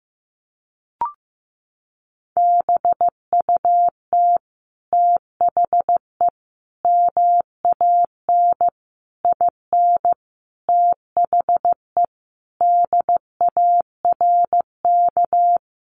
Morse code, heard as a keyed tone, about 14 wpm, 700 Hz. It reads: BUT THE MAN IN THE DARK
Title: Sentences from Top 600 Words 15wpm